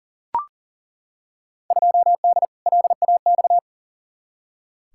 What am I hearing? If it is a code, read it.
Morse code, heard as a keyed tone, about 38 wpm, 700 Hz. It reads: 2D LAX
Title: Field Day Exchanges - 40wpm